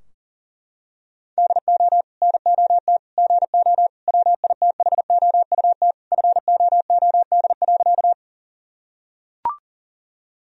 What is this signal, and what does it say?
Morse code, heard as a keyed tone, about 40 wpm, 700 Hz. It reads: DO NOT GO WITHOUT FOOD.
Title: Sentences from Top 1000 Words - Review Part 3 - 40wpm